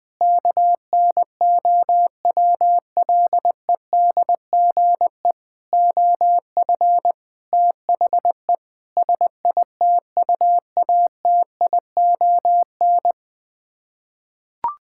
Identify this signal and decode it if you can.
Morse code, heard as a keyed tone, about 20 wpm, 700 Hz. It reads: KNOWLEDGE OF THE SITUATION
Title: Sets of 9 Words - 20wpm